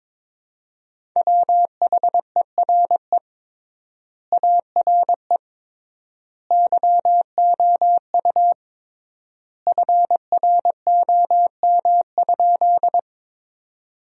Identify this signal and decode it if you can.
Morse code, heard as a keyed tone, about 21 wpm, 700 Hz. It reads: WHERE ARE YOU FROM?
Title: 3x Word-Spacing - Sentences from Top 200 Words - 22wpm